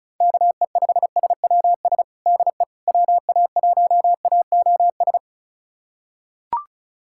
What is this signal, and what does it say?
Morse code, heard as a keyed tone, about 34 wpm, 700 Hz. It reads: KE5SWS DE WA1AOS
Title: USA Callsign DE Callsign - 6 Characters - 35wpm